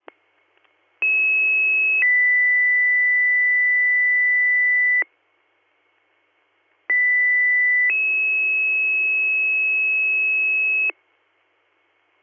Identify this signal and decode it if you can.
Two-tone page, heard as a one-second tone, then a three-second tone, about 2600 then 2000 Hz, about 2000 then 2600 Hz. It is heard from 1.0 to 10.9 s.